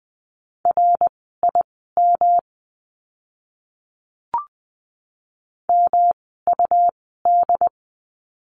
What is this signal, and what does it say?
Morse code, heard as a keyed tone, about 20 wpm, spaced as for 14 wpm, 700 Hz. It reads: RIM MUD
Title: 108 Lesson 18f - Letter m - Words 20wpm